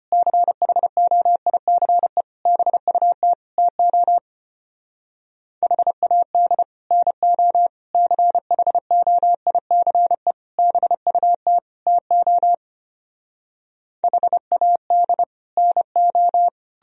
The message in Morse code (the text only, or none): CHOICE BUT TO HAD NO CHOICE BUT TO HAD NO